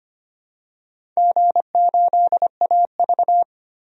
Morse code, heard as a keyed tone, about 25 wpm, 700 Hz. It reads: G8AV
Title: DXCC-100 Call Signs - 4 Characters - Repeated 3x - 25wpm